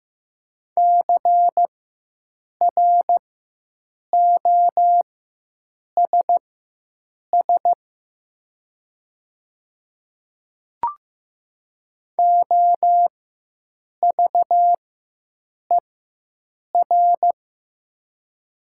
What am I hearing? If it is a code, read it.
Morse code, heard as a keyed tone, about 15 wpm, spaced as for 7 wpm, 700 Hz. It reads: CROSS OVER
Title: 4x Character Spacing - Top 500 Words - 15wpm